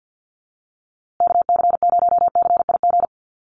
Morse code, heard as a keyed tone, about 50 wpm, 700 Hz. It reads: KC0CIG
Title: USA Call Signs Repeated 2x 50wpm